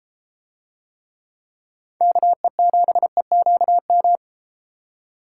Morse code, heard as a keyed tone, about 32 wpm, 700 Hz. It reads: KE7EQM